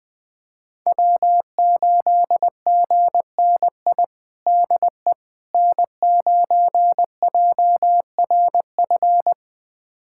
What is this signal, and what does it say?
Morse code, heard as a keyed tone, about 20 wpm, 700 Hz. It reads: W8GNI DE N9JRF
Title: USA Callsign DE Callsign - 5 Characters - 20wpm